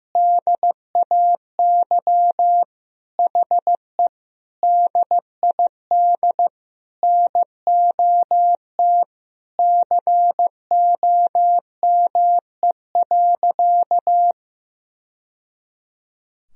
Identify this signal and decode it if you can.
Morse code, heard as a keyed tone, about 15 wpm, 700 Hz. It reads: DAY HE DID NOT COME.
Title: Sentences from Top 1000 Words - Review Part 4 - 15wpm